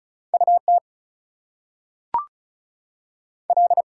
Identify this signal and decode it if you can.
Morse code, heard as a keyed tone, about 36 wpm, 700 Hz. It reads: U T L
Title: US States and Canadian Provinces 35wpm